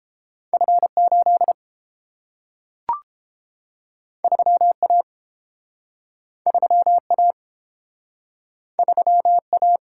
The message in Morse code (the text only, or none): F8 3A 3A 3A